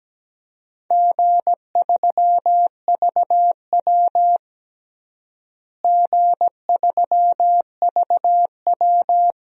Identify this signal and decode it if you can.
Morse code, heard as a keyed tone, about 17 wpm, 700 Hz. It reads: G3VW G3VW
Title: DXCC-100 Call Signs - 4 Characters - Repeated 2x - 17wpm